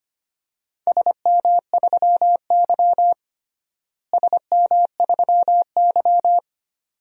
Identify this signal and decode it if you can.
Morse code, heard as a keyed tone, about 25 wpm, 700 Hz. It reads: SM3Y SM3Y
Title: DXCC-100 Call Signs - 4 Characters - Repeated 2x - 25wpm